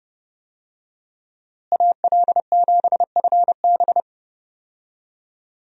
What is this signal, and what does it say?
Morse code, heard as a keyed tone, about 30 wpm, 700 Hz. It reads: AL7FB
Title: USA Call Signs - 5 Characters - Repeated 2x - 30wpm